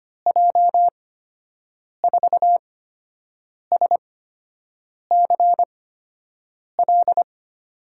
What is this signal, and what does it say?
Morse code, heard as a keyed tone, about 25 wpm, spaced as for 7 wpm, 700 Hz. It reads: J4SCL
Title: USA Call Signs - 6 Characters - 8x Character Spacing - 25wpm